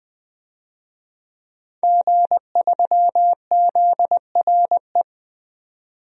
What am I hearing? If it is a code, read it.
Morse code, heard as a keyed tone, about 20 wpm, 700 Hz. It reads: G3ZRE
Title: DXCC-100 Call Signs - 5 Characters - 20wpm